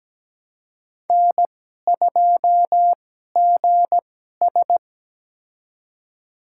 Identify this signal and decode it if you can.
Morse code, heard as a keyed tone, about 17 wpm, spaced as for 12 wpm, 700 Hz. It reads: N2GS
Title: USA Call Signs - 2x Character Spacing - 17wpm